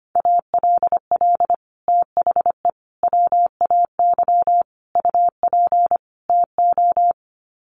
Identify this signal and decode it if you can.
Morse code, heard as a keyed tone, about 25 wpm, 700 Hz. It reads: ALL THE WAY UP TO
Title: Sets of 5 Words Encore 25wpm